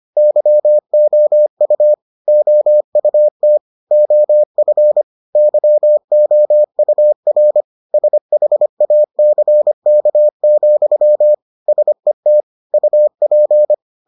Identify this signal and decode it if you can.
Morse code, heard as a keyed tone, about 25 wpm, 600 Hz. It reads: YOU OUT OF YOUR SHACK, SET UP